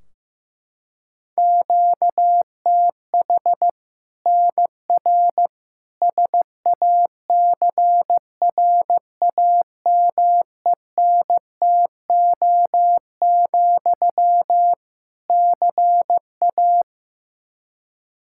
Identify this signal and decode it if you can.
Morse code, heard as a keyed tone, about 16 wpm, 700 Hz. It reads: QTH NR SACRAMENTO, CA
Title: Practice QSO - Exchange 1 - Beginning - 15wpm